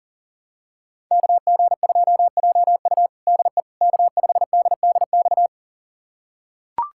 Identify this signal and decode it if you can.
Morse code, heard as a keyed tone, about 40 wpm, 700 Hz. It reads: KG2JU DE K5DDX
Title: USA Callsign DE Callsign - 5 Characters - 40wpm